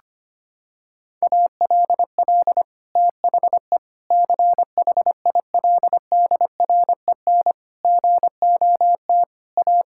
Morse code, heard as a keyed tone, about 25 wpm, 700 Hz. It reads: ALL THE CHILDREN GOT A
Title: Sentences from Top 1000 Words - Review Part 6 - 25wpm